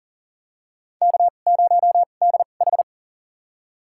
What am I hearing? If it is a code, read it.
Morse code, heard as a keyed tone, about 40 wpm, spaced as for 29 wpm, 700 Hz. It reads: K0DH
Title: USA Call Signs - 2x Character Spacing - 40wpm